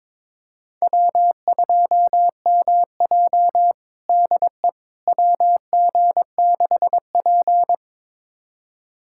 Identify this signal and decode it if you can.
Morse code, heard as a keyed tone, about 22 wpm, 700 Hz. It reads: W2MJ DE WG6P
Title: USA Callsign DE Callsign - 4 Characters - 22wpm